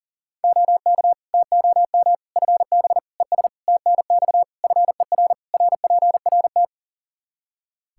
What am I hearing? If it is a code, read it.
Morse code, heard as a keyed tone, about 40 wpm, 700 Hz. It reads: OK TOM FB ES TNX FER RPRT
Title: Practice QSO - Exchange 2 - Info - 40wpm